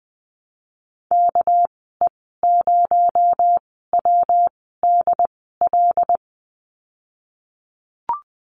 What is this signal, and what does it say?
Morse code, heard as a keyed tone, about 20 wpm, spaced as for 15 wpm, 700 Hz. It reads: KE0WDL
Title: 219 Lesson 35g - Number 0 - Call Signs 20wpm